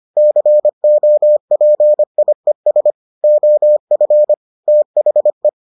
Morse code, heard as a keyed tone, about 25 wpm, 600 Hz. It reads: COPIES OF THE